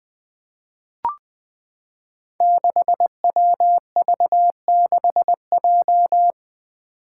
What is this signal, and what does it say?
Morse code, heard as a keyed tone, about 20 wpm, 700 Hz. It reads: 6WV6J
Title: Five Challenging Letter-Number Combinations 20wpm